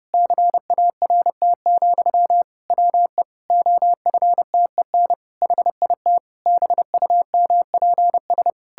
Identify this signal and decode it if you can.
Morse code, heard as a keyed tone, about 32 wpm, 700 Hz. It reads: CART, WE OFTEN HIT BUMPS
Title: Pangram Challenge v2 - 30wpm